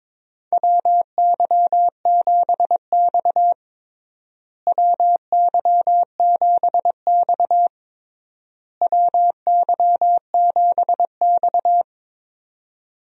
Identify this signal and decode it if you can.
Morse code, heard as a keyed tone, about 22 wpm, 700 Hz. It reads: WY7X WY7X WY7X
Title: USA Call Signs - 4 Characters - Repeated 3x - 22wpm